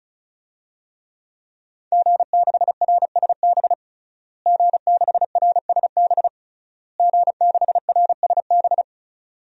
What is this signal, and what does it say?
Morse code, heard as a keyed tone, about 34 wpm, 700 Hz. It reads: G6RSB G6RSB G6RSB
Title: DXCC-100 Call Signs - 5 Characters - Repeated 3x - 35wpm